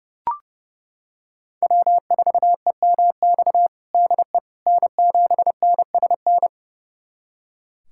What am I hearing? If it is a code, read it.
Morse code, heard as a keyed tone, about 30 wpm, 700 Hz. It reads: W4EMX DE N7NSN
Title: USA Callsign DE Callsign - 5 Characters - 30wpm